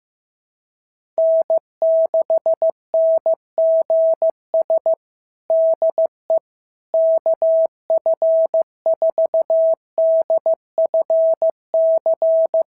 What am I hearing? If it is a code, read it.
Morse code, heard as a keyed tone, about 15 wpm, 650 Hz. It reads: N6NGS DE KF4DFC
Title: SOTA QSO Practice 15wpm